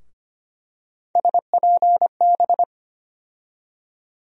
Morse code, heard as a keyed tone, about 25 wpm, 700 Hz. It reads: SPB
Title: USA Call Sign Postfixes - 3 Letters 25wpm